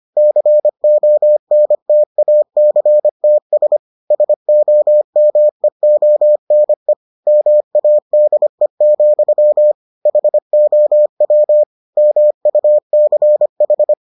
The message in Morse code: CONTACTS SOMEONE MADE, HOW MUCH

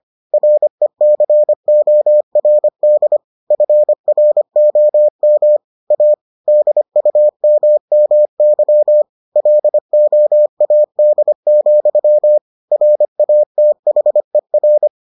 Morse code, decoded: RECORD FROM A DUMMY LOAD, RATHER

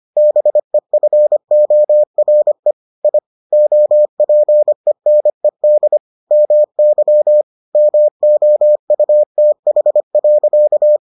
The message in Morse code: BEFORE I OPENED MY MOUTH.

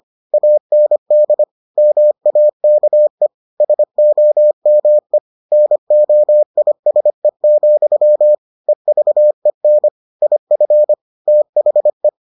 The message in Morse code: AND MAKE SOME NOISE, EVEN IF THE

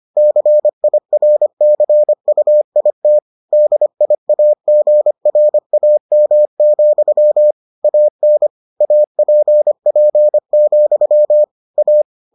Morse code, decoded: CIRCUIT DIAGRAM, AN APP, A

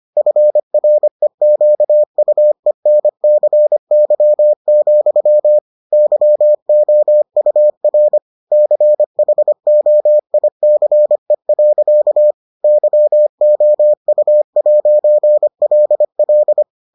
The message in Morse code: FREQUENCY, YOUR CHOICE. YOU'LL

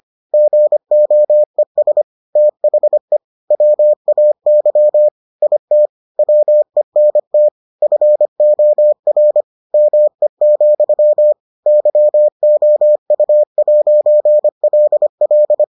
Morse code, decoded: GOES THE WAY IT WENT FOR ME, YOU'LL

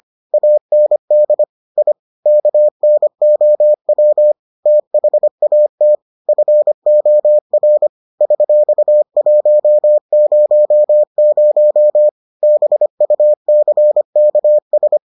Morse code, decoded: AND I KNOW THAT FOR $100 BUCKS

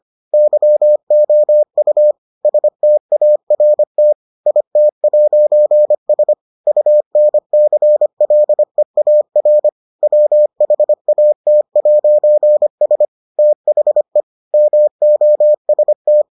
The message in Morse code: YOU START IT'S UNCLEAR WHAT'S THE MOST

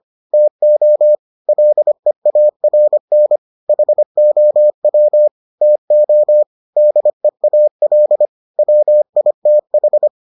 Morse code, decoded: TO LEARN HOW TO DEAL WITH

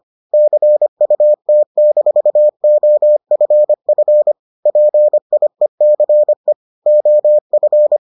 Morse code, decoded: CUT-OFF PIECE OF

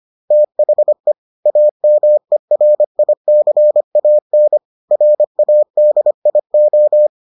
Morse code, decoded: THE AMERICAN RADIO

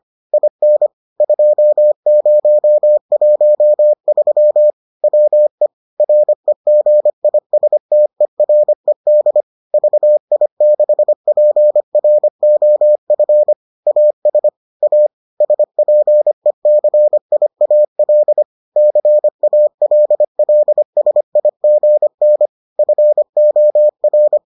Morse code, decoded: IN 2013 WE REGISTERED VI6PROF AS A SPECIAL CALLSIGN FOR

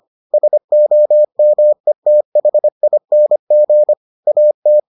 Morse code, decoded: SOMETHING AT

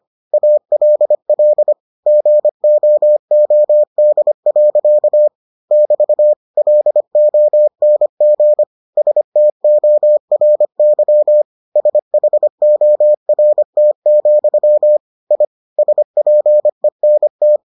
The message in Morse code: ALL GOOD. <BT> LONG STORY SHORT, I SPENT

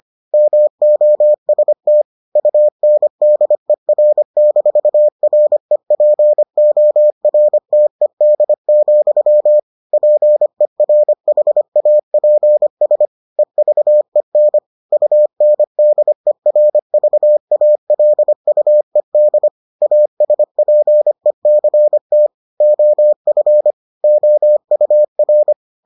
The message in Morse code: MOST UNDER-REPORTED, PERHAPS EVEN UNDERVALUED ASPECT OF OUR

